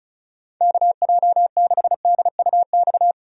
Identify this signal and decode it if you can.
Morse code, heard as a keyed tone, about 36 wpm, 700 Hz. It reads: KJ6DUX
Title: USA Call Signs - 6 Characters - 35wpm